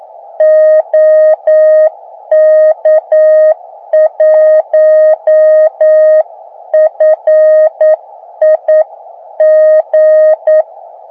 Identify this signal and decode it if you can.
Morse code, heard as a keyed tone, about 9 wpm, 620 Hz. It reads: OK1FIG